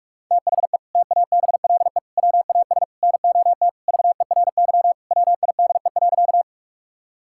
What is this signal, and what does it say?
Morse code, heard as a keyed tone, about 46 wpm, 690 Hz. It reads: THE TABLE WAS NOT VERY WIDE.
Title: Sentences from Top 1000 Words - Review Part 6 - 45wpm